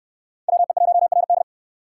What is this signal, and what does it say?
Morse code, heard as a keyed tone, about 64 wpm, 690 Hz. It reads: WE1RD